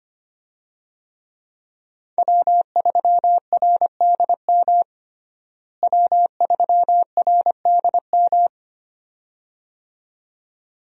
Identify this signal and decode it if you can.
Morse code, heard as a keyed tone, about 25 wpm, 700 Hz. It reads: W3RDM W3RDM
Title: USA Call Signs - 5 Characters - Repeated 2x - 25wpm